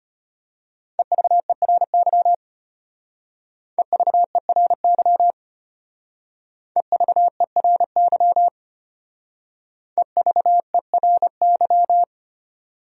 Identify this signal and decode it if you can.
Morse code, heard as a keyed tone, about 31 wpm, 700 Hz. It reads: EVERY EVERY EVERY EVERY
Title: Speed-Racing Top 300 Words 25wpm